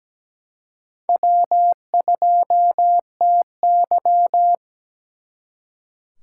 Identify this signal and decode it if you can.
Morse code, heard as a keyed tone, about 20 wpm, 700 Hz. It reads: W2TY
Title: Speed Racing - USA 1x2 Call Signs 17wpm